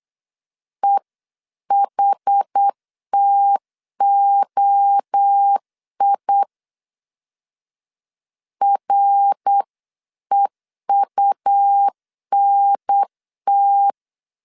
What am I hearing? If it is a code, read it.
Morse code, heard as a keyed tone, about 8 wpm, 800 Hz. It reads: EHTOI REUNT
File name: CW / DET / 1 3 1500C S C 8wpm 4214